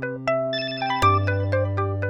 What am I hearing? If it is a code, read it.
There is no code.